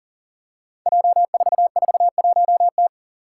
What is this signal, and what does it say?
Morse code, heard as a keyed tone, about 37 wpm, 700 Hz. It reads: J441T
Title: Five Challenging Letter-Number Combinations 40wpm